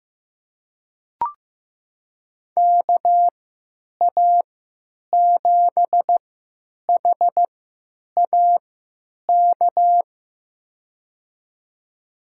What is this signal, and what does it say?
Morse code, heard as a keyed tone, about 15 wpm, spaced as for 8 wpm, 700 Hz. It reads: KA7HAK
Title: USA Call Signs - 3x Character Spacing - 15wpm